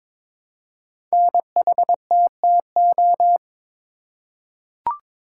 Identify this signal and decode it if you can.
Morse code, heard as a keyed tone, about 22 wpm, 700 Hz. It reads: NHTTO
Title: Code Groups - 5 Letters - 22wpm